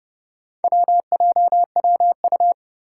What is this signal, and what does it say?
Morse code, heard as a keyed tone, about 32 wpm, 700 Hz. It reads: WJWU
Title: Four Challenging Letter-Number Combinations 30wpm